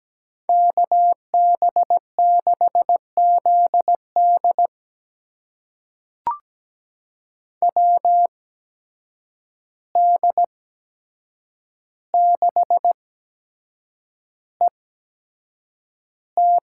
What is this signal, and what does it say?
Morse code, heard as a keyed tone, about 17 wpm, spaced as for 7 wpm, 700 Hz. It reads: KB6ZD WD6ET
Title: USA Call Signs - 8x Character Spacing - 17wpm